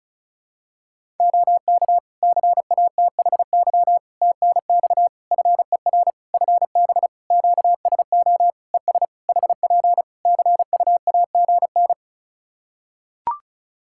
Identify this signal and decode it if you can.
Morse code, heard as a keyed tone, about 36 wpm, 700 Hz. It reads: OK CATHY TNX FER FB QSO ES HP CUAGN
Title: Practice QSO - Exchange 4 - Ending - 35wpm